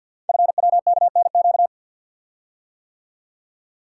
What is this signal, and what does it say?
Morse code, heard as a keyed tone, about 49 wpm, 700 Hz. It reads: FWKNQ
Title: Code Groups - 5 Letters - 50wpm